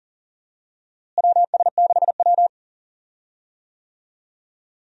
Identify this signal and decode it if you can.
Morse code, heard as a keyed tone, about 40 wpm, 700 Hz. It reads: WS6W
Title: USA Call Signs - 4 Characters - Repeated 3x - 40wpm